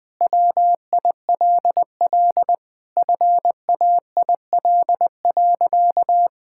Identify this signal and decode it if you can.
Morse code, heard as a keyed tone, about 20 wpm, 700 Hz. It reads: WILL FAIL.
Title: Sentences with 7-Word Phrases - Spoken Beginning - 20wpm